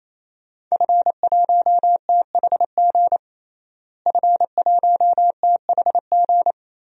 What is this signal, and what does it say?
Morse code, heard as a keyed tone, about 28 wpm, 700 Hz. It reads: F1THG F1THG
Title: DXCC-100 Call Signs - 5 Characters - Repeated 3x - 28wpm